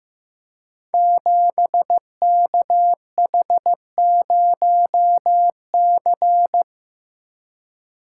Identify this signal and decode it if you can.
Morse code, heard as a keyed tone, about 15 wpm, 700 Hz. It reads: 7KH0C
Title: Code Groups - 5 Letter:Numbers - 15wpm